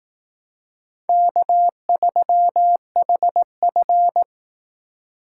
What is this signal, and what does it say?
Morse code, heard as a keyed tone, about 18 wpm, 700 Hz. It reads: K3HF
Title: Speed Racing - USA Call Signs 15wpm